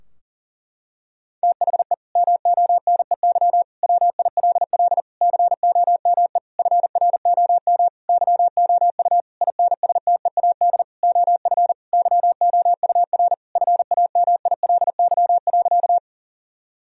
Morse code, heard as a keyed tone, about 37 wpm, 700 Hz. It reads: THE MONEY WILL COME FROM YOU INSTEAD OF YOUR FAMILY.
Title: Sentences from Top 1000 Words - Review Part 4 - 40wpm